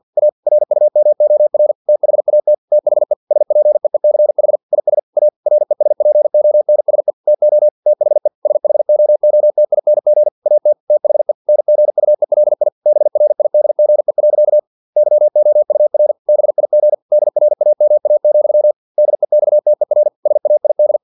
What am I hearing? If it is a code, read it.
Morse code, heard as a keyed tone, about 50 wpm, 600 Hz. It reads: A RUMOR THAT THE SPEECH IS A RESPONSE TO THE SHOOTING AT THE NGUELI BRIDGE. YOUR BIG DRAMA, DEXTER SAID
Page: Never 49wpm